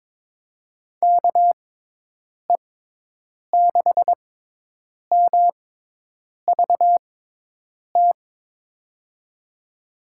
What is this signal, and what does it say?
Morse code, heard as a keyed tone, about 22 wpm, spaced as for 8 wpm, 700 Hz. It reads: KE6MVT
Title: USA Call Signs - 6 Characters - 6x Character Spacing - 22wpm